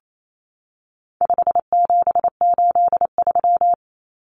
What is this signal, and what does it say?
Morse code, heard as a keyed tone, about 28 wpm, 700 Hz. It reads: 5783
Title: Numbers - 4-Digit 28wpm